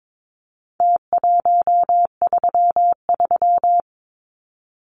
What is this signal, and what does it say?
Morse code, heard as a keyed tone, about 22 wpm, 700 Hz. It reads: T133
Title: Numbers - Contest Style - 4-Digit 22wpm